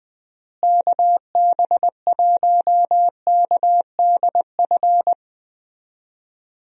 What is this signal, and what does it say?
Morse code, heard as a keyed tone, about 20 wpm, 700 Hz. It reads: KB1KDF
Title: USA Call Signs - 6 Characters - Repeated 3x - 20wpm